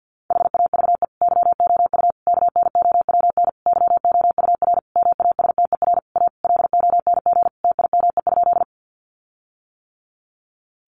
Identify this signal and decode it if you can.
Morse code, heard as a keyed tone, about 50 wpm, 700 Hz. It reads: HAVE YOU KNOWN YOUR MASTER A LONG TIME?
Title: Sentences from Top 1300 Words 50wpm